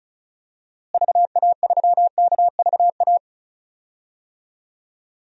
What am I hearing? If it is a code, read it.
Morse code, heard as a keyed tone, about 36 wpm, 700 Hz. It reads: VA3KVA
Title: DXCC-100 Call Signs - 6 Characters - Repeated 2x - 35wpm